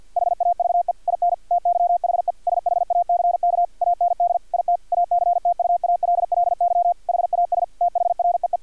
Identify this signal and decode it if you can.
Morse code, heard as a keyed tone, about 50 wpm, 700 Hz. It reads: HAVE IN TYHE SHACK AND IT ACTUALLY HAS THREE
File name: archived / CRSnet